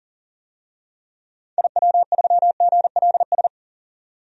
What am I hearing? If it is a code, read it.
Morse code, heard as a keyed tone, about 40 wpm, 700 Hz. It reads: IW3GLS
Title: DXCC-100 Call Signs - 6 Characters - 40wpm